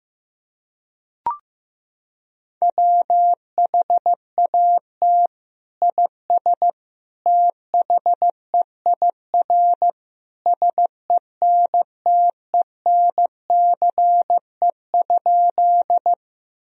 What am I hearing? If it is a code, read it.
Morse code, heard as a keyed tone, about 15 wpm, 700 Hz. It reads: WHAT IS THEIR SENTENCE?
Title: Sentences from Top 1000 Words - Review Part 1 - 15wpm